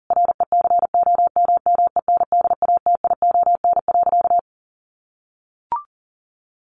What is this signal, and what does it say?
Morse code, heard as a keyed tone, about 40 wpm, 700 Hz. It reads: RECOMMENDATION.
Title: Sentences from Top 1400 Words 40wpm